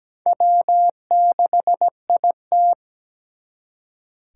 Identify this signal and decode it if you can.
Morse code, heard as a keyed tone, about 20 wpm, 700 Hz. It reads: W6IT
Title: Speed Racing - USA 1x2 Call Signs 17wpm